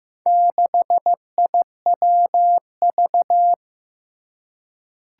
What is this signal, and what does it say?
Morse code, heard as a keyed tone, about 15 wpm, 700 Hz. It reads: 6IWV
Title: Four Challenging Letter-Number Combinations 15wpm